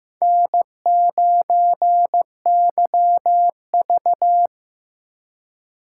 Morse code, heard as a keyed tone, about 15 wpm, 700 Hz. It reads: N9YV
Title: USA Call Signs - 4 Characters - 8x Character Spacing - 15wpm